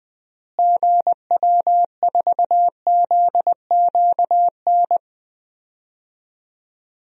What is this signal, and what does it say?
Morse code, heard as a keyed tone, about 21 wpm, 700 Hz. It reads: GW4ZQN
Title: DXCC-100 Call Signs - 6 Characters - Repeated 3x - 20wpm